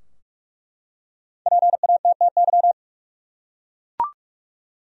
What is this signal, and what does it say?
Morse code, heard as a keyed tone, about 47 wpm, 700 Hz. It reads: PATTY
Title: USA Common Names - Max 5 Letters - 45wpm